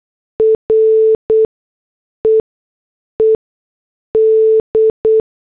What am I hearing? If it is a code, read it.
Morse code, heard as a keyed tone, about 8 wpm, spaced as for 6 wpm, 440 Hz. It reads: REED